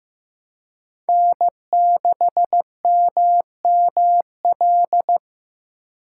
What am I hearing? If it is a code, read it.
Morse code, heard as a keyed tone, about 15 wpm, 700 Hz. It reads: N6MML
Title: USA Call Signs - 5 Characters - 2x Character Spacing - 15wpm